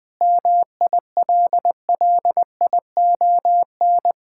Morse code, heard as a keyed tone, about 20 wpm, 700 Hz. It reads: MILLION
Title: Sentences with 9-Word Phrases - Spoken Beginning - 20wpm